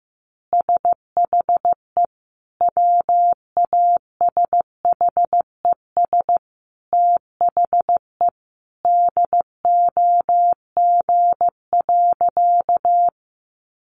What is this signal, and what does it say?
Morse code, heard as a keyed tone, about 14 wpm, 700 Hz. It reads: SHE WASHES THE DOG.
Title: Easy Phrases 15wpm